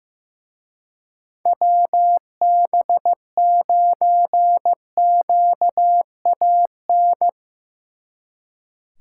Morse code, heard as a keyed tone, about 17 wpm, 700 Hz. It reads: WB9QAN
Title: Speed Racing - USA 2x3 Call Signs 15wpm